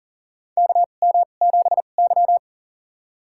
Morse code, heard as a keyed tone, about 40 wpm, spaced as for 29 wpm, 700 Hz. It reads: KM7Y
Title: USA Call Signs - 4 Characters - 2x Character Spacing - 40wpm